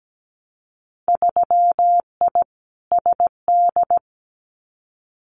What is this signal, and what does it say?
Morse code, heard as a keyed tone, about 17 wpm, 700 Hz. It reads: 3I SD
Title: Winter Field Day Exchanges Encore 17wpm